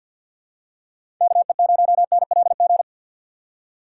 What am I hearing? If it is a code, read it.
Morse code, heard as a keyed tone, about 47 wpm, 700 Hz. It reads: KE0NLG
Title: USA Call Signs - 6 Characters - 4x Character Spacing - 50wpm